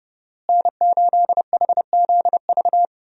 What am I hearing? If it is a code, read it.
Morse code, heard as a keyed tone, about 30 wpm, 700 Hz. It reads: N8HZV
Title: USA Call Signs - 5 Characters - Repeated 3x - 30wpm